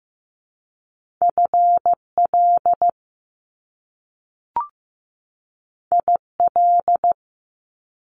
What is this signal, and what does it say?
Morse code, heard as a keyed tone, about 15 wpm, 700 Hz. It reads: FL IL
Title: Top Two Letter Combinations in Words 15wpm